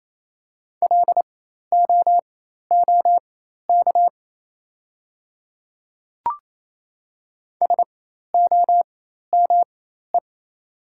Morse code, heard as a keyed tone, about 28 wpm, spaced as for 13 wpm, 700 Hz. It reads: LOOK SOME
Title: 4x Character Spacing - Top 100 Words - 28wpm